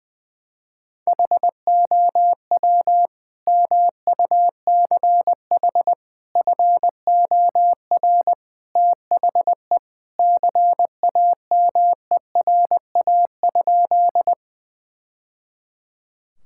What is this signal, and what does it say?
Morse code, heard as a keyed tone, about 21 wpm, 700 Hz. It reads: HOW MUCH FOR THE CAMERA?
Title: Sentences from Top 1000 Words - Review Part 5 - 20wpm